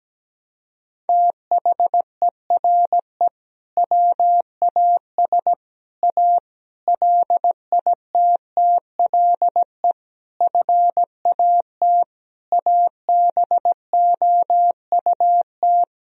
Morse code, heard as a keyed tone, about 17 wpm, 700 Hz. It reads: THERE WAS A LITTLE FAT ABOUT
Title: Sentences from Top 1000 Words - Review Part 4 - 17wpm